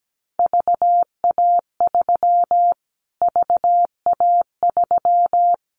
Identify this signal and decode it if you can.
Morse code, heard as a keyed tone, about 17 wpm, 700 Hz. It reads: VA3 VA3
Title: Callsign Prefixes for Top 25 DXCC - Repeated 3x - 17wpm